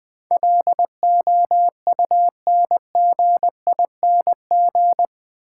Morse code, heard as a keyed tone, about 22 wpm, 700 Hz. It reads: LOUNGING
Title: Easy Words - Speed-Racing with Hints - 20wpm